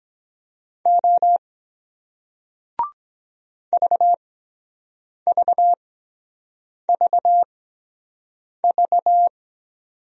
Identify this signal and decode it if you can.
Morse code, heard as a keyed tone, about 23 wpm, spaced as for 7 wpm, 700 Hz. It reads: O VVVV